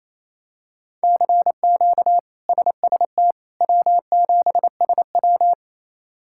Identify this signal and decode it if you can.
Morse code, heard as a keyed tone, about 27 wpm, 700 Hz. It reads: CQ SST W7SW
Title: K1USN SST QSO Practice - 28wpm